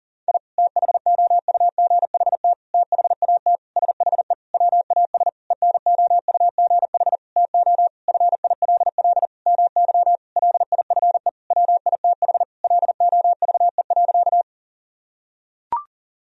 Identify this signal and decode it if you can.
Morse code, heard as a keyed tone, about 40 wpm, 700 Hz. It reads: I THOUGHT THAT SHE WAS ENOUGH TO FILL MY LIFE WITH LOVE.
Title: Sentences from Top 1000 Words - Review Part 3 - 40wpm